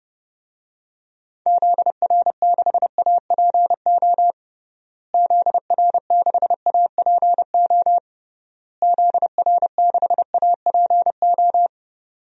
Morse code, heard as a keyed tone, about 30 wpm, 700 Hz. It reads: ZR6APO ZR6APO ZR6APO
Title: DXCC-100 Call Signs - 6 Characters - Repeated 3x - 30wpm